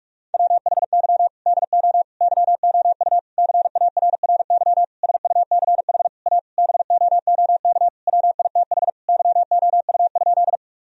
Morse code, heard as a keyed tone, about 44 wpm, 700 Hz. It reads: WHY DO YOU CARRY SUCH A BOOK WITH YOU?
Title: Sentences from Top 1000 Words - Review Part 1 - 45wpm